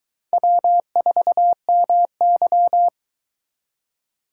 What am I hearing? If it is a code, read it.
Morse code, heard as a keyed tone, about 22 wpm, 700 Hz. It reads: W4MY